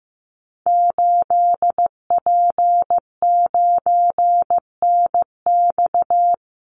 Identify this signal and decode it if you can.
Morse code, heard as a keyed tone, about 15 wpm, 700 Hz. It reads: 8P9NX
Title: Call Signs Repeated 2x - 15wpm